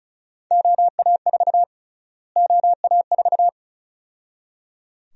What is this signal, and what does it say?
Morse code, heard as a keyed tone, about 34 wpm, 700 Hz. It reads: OA4 OA4
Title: USA Call Sign Prefixes - Repeated 2x - 35wpm